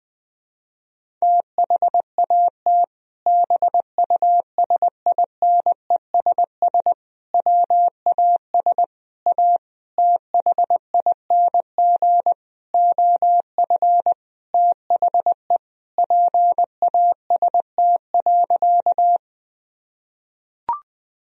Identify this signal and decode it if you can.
Morse code, heard as a keyed tone, about 20 wpm, 700 Hz. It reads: THAT BUSINESS WAS A THING OF THE PAST.
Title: Sentences from Top 1000 Words - Review Part 3 - 20wpm